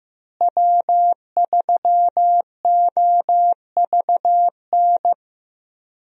Morse code, heard as a keyed tone, about 15 wpm, 700 Hz. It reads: W3OVN
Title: USA Call Signs - 5 Characters - Repeated 3x - 15wpm